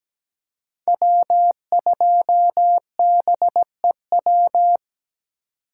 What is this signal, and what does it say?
Morse code, heard as a keyed tone, about 17 wpm, 700 Hz. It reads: W2BEW
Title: USA Call Signs - 5 Characters - 4x Character Spacing - 17wpm